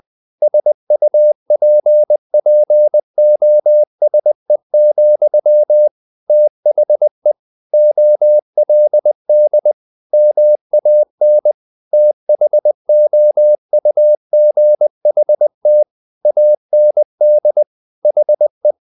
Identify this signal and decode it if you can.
Morse code, heard as a keyed tone, about 20 wpm, 600 Hz. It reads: SUPPOSE, THE OLD MAN THOUGHT AND HE
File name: the old man and the sea / the old man and the sea part 1 20wpm